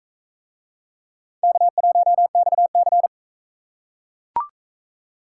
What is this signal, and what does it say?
Morse code, heard as a keyed tone, about 42 wpm, 700 Hz. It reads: K1XC